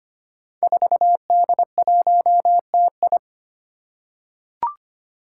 Morse code, heard as a keyed tone, about 25 wpm, 700 Hz. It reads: 4D1TI
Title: Five Challenging Letter-Number Combinations 25wpm